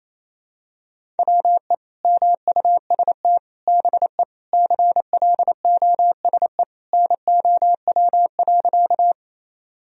Morse code, heard as a keyed tone, about 28 wpm, 700 Hz. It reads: WE MUST BE CLOSE NOW.
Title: Sentences from Top 1000 Words - Review Part 1 - 28wpm